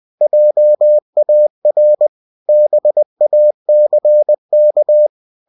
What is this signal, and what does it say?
Morse code, heard as a keyed tone, about 20 wpm, 600 Hz. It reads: JAR BACK
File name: cw-books / winnie-the-pooh / winnie-20wpm-page-0038